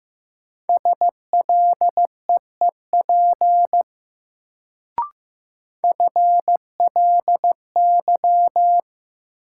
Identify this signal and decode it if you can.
Morse code, heard as a keyed tone, about 15 wpm, 700 Hz. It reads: SLEEP FLY
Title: Sentences from Top 500 Words - New Word Drill - 15wpm